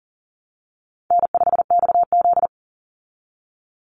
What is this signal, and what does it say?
Morse code, heard as a keyed tone, about 40 wpm, 700 Hz. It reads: N5XZ
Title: Easy Call Signs - 40wpm